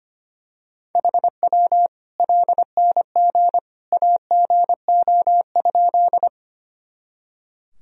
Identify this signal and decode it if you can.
Morse code, heard as a keyed tone, about 25 wpm, 700 Hz. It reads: HW LNG AGO?
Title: Sentences from Top 500 Words and Abbreviations 25wpm